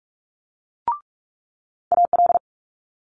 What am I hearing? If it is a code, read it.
Morse code, heard as a keyed tone, about 46 wpm, 700 Hz. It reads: AL
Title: Multi-letter Phonemes Quick Fire 45wpm